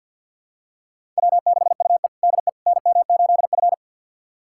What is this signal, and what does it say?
Morse code, heard as a keyed tone, about 52 wpm, 700 Hz. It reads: W6UE DE NM8F
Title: USA Callsign DE Callsign - 4 Characters - 50wpm